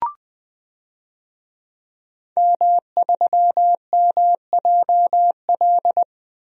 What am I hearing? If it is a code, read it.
Morse code, heard as a keyed tone, about 21 wpm, 700 Hz. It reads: M3MJL
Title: DXCC-100 Call Signs - 5 Characters - Repeated 2x - 20wpm